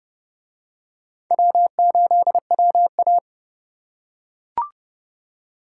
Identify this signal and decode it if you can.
Morse code, heard as a keyed tone, about 27 wpm, 700 Hz. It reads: W8WA